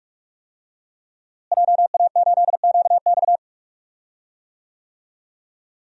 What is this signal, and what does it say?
Morse code, heard as a keyed tone, about 45 wpm, 700 Hz. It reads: JA8QX
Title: DXCC-100 Call Signs - 5 Characters - Repeated 2x - 45wpm